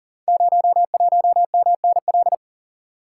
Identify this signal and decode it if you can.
Morse code, heard as a keyed tone, about 38 wpm, 700 Hz. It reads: 01MNL
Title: Code Groups - 5 Letter:Numbers - 40wpm